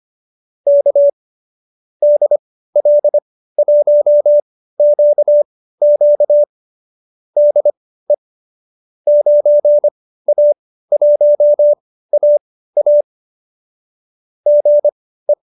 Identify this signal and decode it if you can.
Morse code, heard as a keyed tone, about 25 wpm, spaced as for 15 wpm, 590 Hz. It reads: K DL1QQ DE 9A1AA GE
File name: qso219b 15